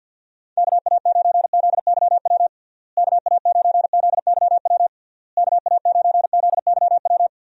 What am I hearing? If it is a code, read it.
Morse code, heard as a keyed tone, about 52 wpm, 700 Hz. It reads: KA9ZYW KA9ZYW KA9ZYW
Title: USA Call Signs - 6 Characters - Repeated 3x - 50wpm